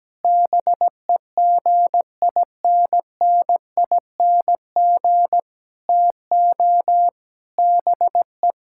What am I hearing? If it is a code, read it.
Morse code, heard as a keyed tone, about 17 wpm, 700 Hz. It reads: BEGINNING TO BE
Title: Sentences from Top 1000 Words - Review Part 5 - 17wpm